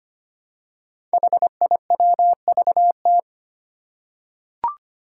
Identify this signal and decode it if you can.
Morse code, heard as a keyed tone, about 25 wpm, 700 Hz. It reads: HIWVT
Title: Five Challenging Letter-Number Combinations 25wpm